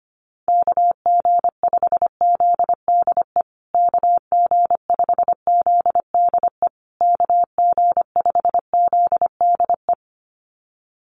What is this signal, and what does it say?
Morse code, heard as a keyed tone, about 25 wpm, 700 Hz. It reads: KG5ZDE KG5ZDE KG5ZDE
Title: USA Call Signs Repeated 3x 25wpm